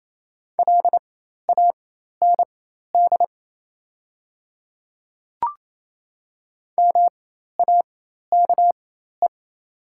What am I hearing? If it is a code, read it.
Morse code, heard as a keyed tone, about 28 wpm, spaced as for 13 wpm, 690 Hz. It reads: LAND MAKE
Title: 4x Character Spacing - Top 300 Words - 28wpm